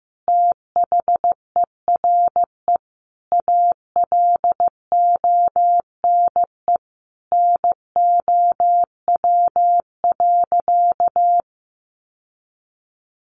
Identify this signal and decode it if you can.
Morse code, heard as a keyed tone, about 15 wpm, 700 Hz. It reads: THERE ALONE NOW.
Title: Sentences from Top 1100 Words 15wpm